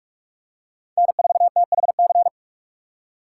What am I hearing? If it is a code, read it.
Morse code, heard as a keyed tone, about 46 wpm, 700 Hz. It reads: N4THC
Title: USA Call Signs - 5 Characters - 3x Character Spacing - 45wpm